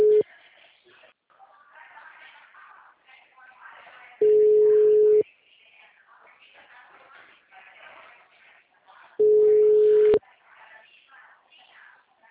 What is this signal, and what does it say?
Ringback tone (Europe, 425 Hz), with speech.